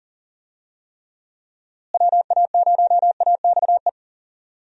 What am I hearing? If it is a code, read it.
Morse code, heard as a keyed tone, about 40 wpm, 700 Hz. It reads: WA0AXE
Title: USA Call Signs - 6 Characters - Repeated 2x - 40wpm